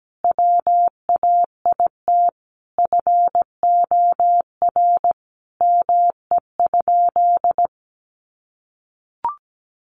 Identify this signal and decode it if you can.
Morse code, heard as a keyed tone, about 17 wpm, 700 Hz. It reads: WAIT FOR ME?
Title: Sentences from Top 500 Words 17wpm